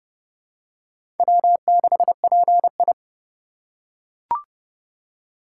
Morse code, heard as a keyed tone, about 30 wpm, 700 Hz. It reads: W6PI
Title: USA Call Signs - 4 Characters - 2x Character Spacing - 30wpm